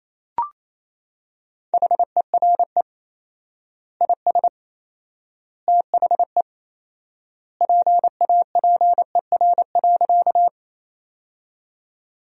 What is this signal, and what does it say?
Morse code, heard as a keyed tone, about 28 wpm, 700 Hz. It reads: HERE IS THE PAPER.
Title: 4x Word-Spacing - Sentences from Top 300 Words - 28wpm